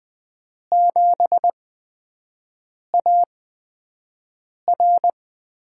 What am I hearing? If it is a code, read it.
Morse code, heard as a keyed tone, about 20 wpm, spaced as for 5 wpm, 700 Hz. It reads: 7AR
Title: USA Call Signs - 8x Character Spacing - 20wpm